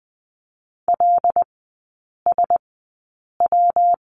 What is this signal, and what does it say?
Morse code, heard as a keyed tone, about 20 wpm, spaced as for 8 wpm, 700 Hz. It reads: LSW
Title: 138 Lesson 23e - Letter f - Three Characters 20wpm